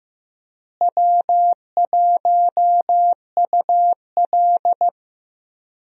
Morse code, heard as a keyed tone, about 15 wpm, 700 Hz. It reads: W1UL
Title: USA Call Signs - 4 Characters - Repeated 2x - 15wpm